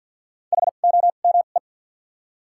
Morse code, heard as a keyed tone, about 47 wpm, spaced as for 35 wpm, 700 Hz. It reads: HOME
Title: 2x Character Spacing - Top 200 Words - 50wpm